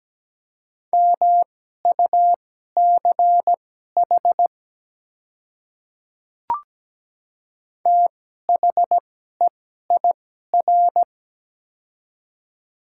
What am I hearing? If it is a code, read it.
Morse code, heard as a keyed tone, about 17 wpm, spaced as for 12 wpm, 700 Hz. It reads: MUCH THEIR
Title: 2x Character Spacing - Top 200 Words - 17wpm